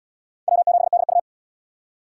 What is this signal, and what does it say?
Morse code, heard as a keyed tone, about 77 wpm, 700 Hz. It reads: W7DU